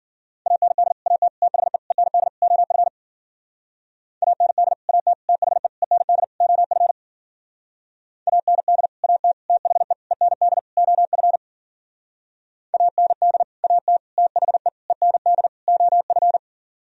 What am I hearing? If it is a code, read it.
Morse code, heard as a keyed tone, about 50 wpm, 700 Hz. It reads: AND AT THE END OF AND AT THE END OF AND AT THE END OF AND AT THE END OF
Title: Speed-Racing - Sets of 5 Words 40wpm